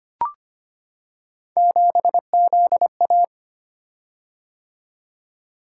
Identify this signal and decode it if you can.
Morse code, heard as a keyed tone, about 25 wpm, 700 Hz. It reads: 7ZA
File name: Three Letter-Number Combinations - 25wpm